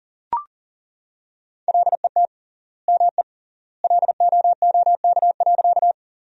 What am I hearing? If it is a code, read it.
Morse code, heard as a keyed tone, about 38 wpm, 700 Hz. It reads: LET ME LOOK.
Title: 3x Word-Spacing - Sentences from Top 300 Words - 40wpm